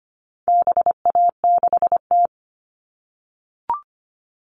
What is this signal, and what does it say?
Morse code, heard as a keyed tone, about 25 wpm, 700 Hz. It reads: BA6T
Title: All Cut Numbers - 4-Digit 25wpm